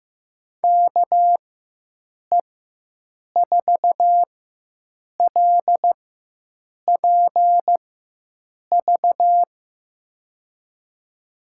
Morse code, heard as a keyed tone, about 15 wpm, spaced as for 7 wpm, 700 Hz. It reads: KE4LPV
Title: USA Call Signs - 4x Character Spacing - 15wpm